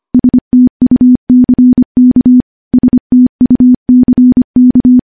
Morse code, heard as a keyed tone, about 25 wpm, 250 Hz. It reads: STUCK STUCK
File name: morse-training-25WPM